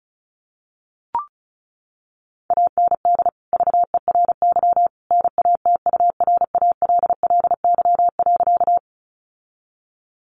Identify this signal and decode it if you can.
Morse code, heard as a keyed tone, about 34 wpm, 700 Hz. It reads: AND VERY NATURALLY.
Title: Sentences from Top 1300 Words 35wpm